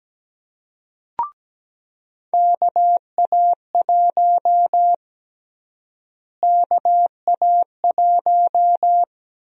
Morse code, heard as a keyed tone, about 17 wpm, 700 Hz. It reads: KA1 KA1
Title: USA Call Sign Prefixes - Repeated 2x - 17wpm